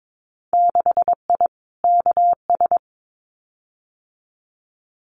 Morse code, heard as a keyed tone, about 22 wpm, 700 Hz. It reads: 6I KS
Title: Winter Field Day Exchanges Encore 22wpm